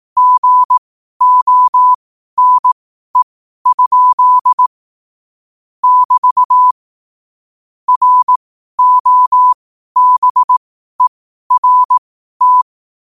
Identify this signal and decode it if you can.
Morse code, heard as a keyed tone, about 18 wpm, spaced as for 12 wpm, 1000 Hz. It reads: GONE? <BT> ROBERT
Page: New quotes every day in morse code at 12 Words per minute.